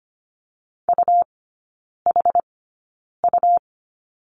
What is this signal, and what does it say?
Morse code, heard as a keyed tone, about 25 wpm, spaced as for 9 wpm, 700 Hz. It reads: UHU
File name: Lesson 17e - Letter u - Three Characters 25wpm